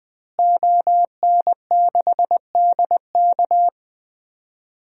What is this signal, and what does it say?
Morse code, heard as a keyed tone, about 20 wpm, 700 Hz. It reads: ON6DK
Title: DXCC-100 Call Signs - 5 Characters - Repeated 3x - 20wpm